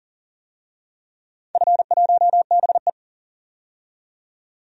Morse code, heard as a keyed tone, about 41 wpm, 690 Hz. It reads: F1BE
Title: DXCC-100 Call Signs - 4 Characters - 40wpm